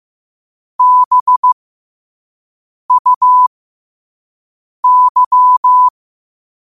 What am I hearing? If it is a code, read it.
Morse code, heard as a keyed tone, about 15 wpm, spaced as for 5 wpm, 1000 Hz. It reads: BUY
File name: qotdfiles / 5wpm